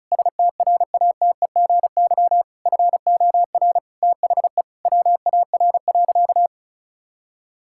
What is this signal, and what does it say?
Morse code, heard as a keyed tone, about 36 wpm, 700 Hz. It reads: STRATEGY FOR THE WAR.
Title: Sentences from Top 1000 Words - Review Part 5 - 35wpm